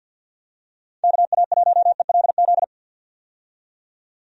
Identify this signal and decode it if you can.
Morse code, heard as a keyed tone, about 50 wpm, 700 Hz. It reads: KA1ELZ
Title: USA Call Signs - 6 Characters - 50wpm